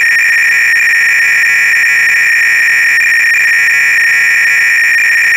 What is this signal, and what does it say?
300-baud modem data.